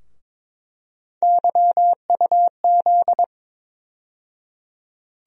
Morse code, heard as a keyed tone, about 22 wpm, 700 Hz. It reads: YUZ
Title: USA Call Sign Postfixes - 3 Letters 22wpm